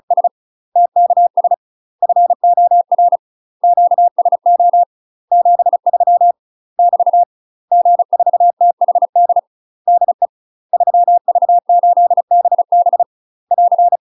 Morse code, heard as a keyed tone, about 36 wpm, 700 Hz. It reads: S TKS FOR QSO 73 <BT> G4THD DE 3V8BB <AR>
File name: UK MASTER QSOs TRAINING / 35WPM / UK MASTER QSOs 0020